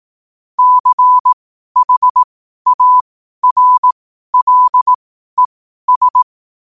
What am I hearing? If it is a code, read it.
Morse code, heard as a keyed tone, about 18 wpm, spaced as for 12 wpm, 1000 Hz. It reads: CHARLES
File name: qotdfiles / 12wpm